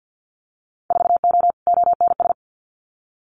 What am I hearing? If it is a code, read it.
Morse code, heard as a keyed tone, about 50 wpm, 700 Hz. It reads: 4O ONS
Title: Winter Field Day Exchanges 50wpm